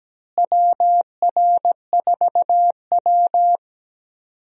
Morse code, heard as a keyed tone, about 17 wpm, 700 Hz. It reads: WR4W
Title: USA Call Signs - 4 Characters - Repeated 3x - 17wpm